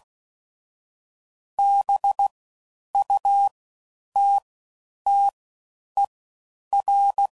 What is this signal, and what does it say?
Morse code, heard as a keyed tone, about 16 wpm, spaced as for 9 wpm, 800 Hz. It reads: BUTTER
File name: Suffixes / Special INGs / Fast Ings